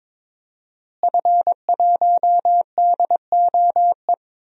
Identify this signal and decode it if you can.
Morse code, heard as a keyed tone, about 22 wpm, 700 Hz. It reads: F1DOE
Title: DXCC-100 Call Signs - 5 Characters - Repeated 2x - 22wpm